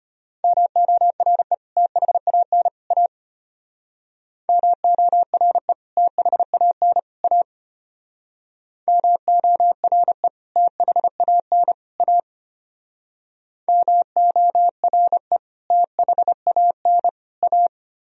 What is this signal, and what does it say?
Morse code, heard as a keyed tone, about 32 wpm, 700 Hz. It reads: MORE THAN A MORE THAN A MORE THAN A MORE THAN A